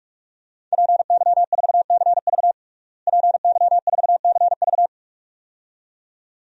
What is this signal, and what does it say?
Morse code, heard as a keyed tone, about 41 wpm, 700 Hz. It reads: PY4CV PY4CV
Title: DXCC-100 Call Signs - 5 Characters - Repeated 2x - 45wpm